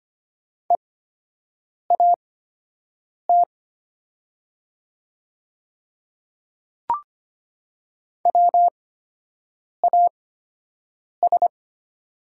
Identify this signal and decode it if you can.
Morse code, heard as a keyed tone, about 25 wpm, spaced as for 7 wpm, 700 Hz. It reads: EAT WAS
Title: 8x Character Spacing - Top 300 Words - 25wpm